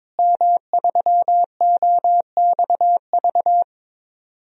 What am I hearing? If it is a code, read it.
Morse code, heard as a keyed tone, about 22 wpm, 700 Hz. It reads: M3OXV
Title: DXCC-100 Call Signs - 5 Characters - Repeated 3x - 22wpm